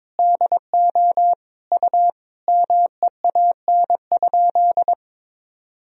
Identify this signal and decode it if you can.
Morse code, heard as a keyed tone, about 22 wpm, 700 Hz. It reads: DO U MEAN?
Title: Sentences from Top 200 Words and Abbreviations 22wpm